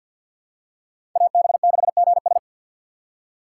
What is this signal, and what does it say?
Morse code, heard as a keyed tone, about 47 wpm, 700 Hz. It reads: AB6GS
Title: USA Call Signs - 5 Characters - 3x Character Spacing - 50wpm